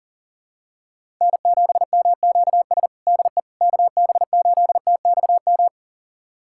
Morse code, heard as a keyed tone, about 37 wpm, 700 Hz. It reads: N7MQS DE KB8TXM
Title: USA Callsign DE Callsign 40wpm